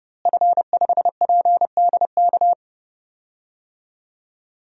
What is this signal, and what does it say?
Morse code, heard as a keyed tone, about 30 wpm, 700 Hz. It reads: F5PDK
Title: DXCC-100 Call Signs - 5 Characters - Repeated 3x - 30wpm